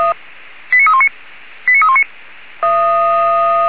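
Five-tone selective call (ZVEI), heard as quick runs of five tones, tones of about 80 ms. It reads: 89319 89319
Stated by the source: ZVEI five-tone call 89315 received by SDR, fire brigade Feuerwehr Orlamünde.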